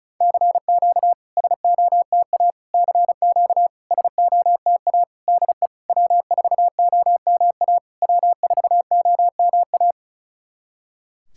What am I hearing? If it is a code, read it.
Morse code, heard as a keyed tone, about 35 wpm, 690 Hz. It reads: CQ SOTA CQ SOTA DE W4OMA W4OMA
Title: SOTA QSO Practice 35wpm